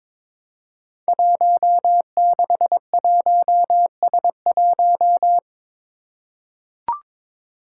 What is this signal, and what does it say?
Morse code, heard as a keyed tone, about 22 wpm, 700 Hz. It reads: 161S1
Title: Five Challenging Letter-Number Combinations 22wpm